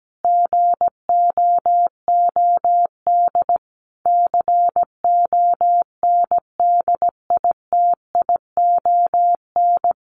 Morse code, heard as a keyed tone, about 17 wpm, 700 Hz. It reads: GOOD CONDITION
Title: Sentences from Top 1500 Words 17wpm